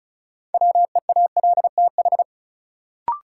Morse code, heard as a keyed tone, about 34 wpm, 700 Hz. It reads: WEALTH
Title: Sentences from Top 1000 Words - Review Part 4 - New Word Drill - 35wpm